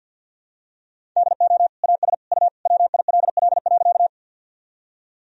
Morse code, heard as a keyed tone, about 50 wpm, 700 Hz. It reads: DO AS U WILL.
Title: Sentences from Top 100 Words and Abbreviations 50wpm